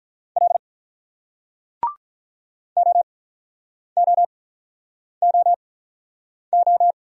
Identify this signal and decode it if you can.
Morse code, heard as a keyed tone, about 48 wpm, spaced as for 9 wpm, 700 Hz. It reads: L OOOO